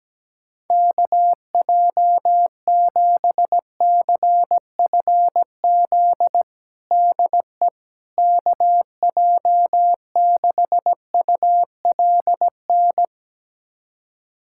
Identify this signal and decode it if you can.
Morse code, heard as a keyed tone, about 17 wpm, 700 Hz. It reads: KJ7CFZ DE KJ6ULN
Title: USA Callsign DE Callsign - 6 Characters - 17wpm